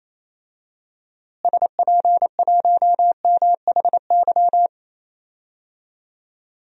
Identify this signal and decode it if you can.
Morse code, heard as a keyed tone, about 28 wpm, 700 Hz. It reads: SP1MHY
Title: DXCC-100 Call Signs - 6 Characters - Repeated 3x - 28wpm